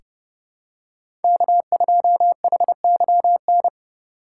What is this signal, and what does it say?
Morse code, heard as a keyed tone, about 32 wpm, 700 Hz. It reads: K2HYN
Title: USA Call Signs - 5 Characters - 4x Character Spacing - 30wpm